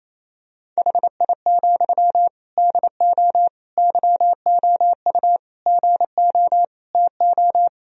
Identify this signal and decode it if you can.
Morse code, heard as a keyed tone, about 28 wpm, 700 Hz. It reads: HI, DO YOU GO TO
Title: Sentences from Top 1000 Words - Review Part 3 - 28wpm